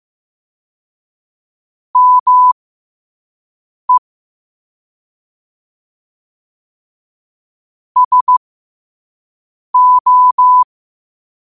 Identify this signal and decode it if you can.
Morse code, heard as a keyed tone, about 15 wpm, spaced as for 5 wpm, 1000 Hz. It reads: ME SO
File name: qotdfiles / 5wpm